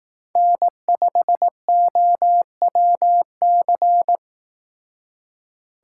Morse code, heard as a keyed tone, about 18 wpm, 700 Hz. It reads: N5OWC
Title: Speed Racing - USA 1x3 Call Signs 15wpm